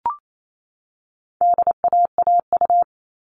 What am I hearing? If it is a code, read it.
Morse code, heard as a keyed tone, about 28 wpm, 700 Hz. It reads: DAAU
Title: All Cut Numbers - 4-Digit 28wpm